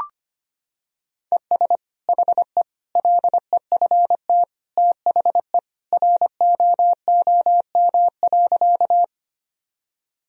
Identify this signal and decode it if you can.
Morse code, heard as a keyed tone, about 25 wpm, 700 Hz. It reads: ES HE LEFT THE ROOM.
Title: Sentences from Top 300 Words and Abbreviations 25wpm